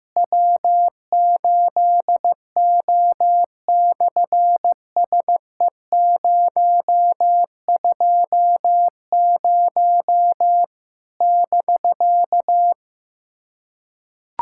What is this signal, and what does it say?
Morse code, heard as a keyed tone, about 15 wpm, 700 Hz. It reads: W8O/SE020 <BK>
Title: SOTA QSO Practice 15wpm